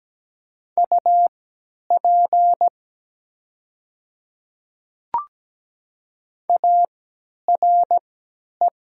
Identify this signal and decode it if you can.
Morse code, heard as a keyed tone, about 17 wpm, spaced as for 10 wpm, 700 Hz. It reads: UP ARE
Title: 3x Character Spacing - Top 100 Words - 17wpm